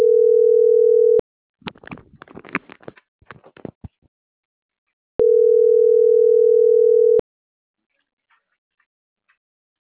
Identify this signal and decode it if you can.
Ringback tone (North America), 440 + 480 Hz.